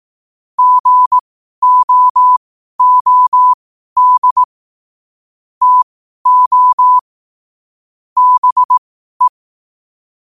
Morse code, heard as a keyed tone, about 18 wpm, spaced as for 12 wpm, 1000 Hz. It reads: GOOD TO BE